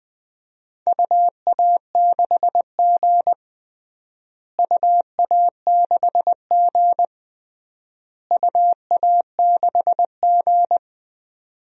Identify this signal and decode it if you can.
Morse code, heard as a keyed tone, about 20 wpm, 700 Hz. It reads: UA6G UA6G UA6G
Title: DXCC-100 Call Signs - 4 Characters - Repeated 3x - 20wpm